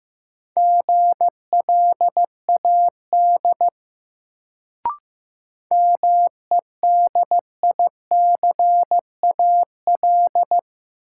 Morse code, heard as a keyed tone, about 15 wpm, 700 Hz. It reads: GLAD MEDICAL
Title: Sentences from Top 800 Words - New Word Drill - 15wpm